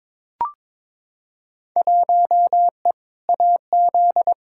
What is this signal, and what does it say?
Morse code, heard as a keyed tone, about 22 wpm, 700 Hz. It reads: 1E AZ
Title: Field Day Exchanges - 22wpm